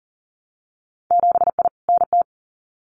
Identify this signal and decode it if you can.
Morse code, heard as a keyed tone, about 38 wpm, 700 Hz. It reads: 7I NT
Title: Winter Field Day Exchanges Encore 40wpm